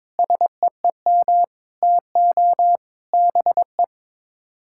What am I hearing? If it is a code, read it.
Morse code, heard as a keyed tone, about 22 wpm, 700 Hz. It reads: SEEM TO BE
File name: practice / 22wpm / Build A Phrase - 5-Word Phrases - 22wpm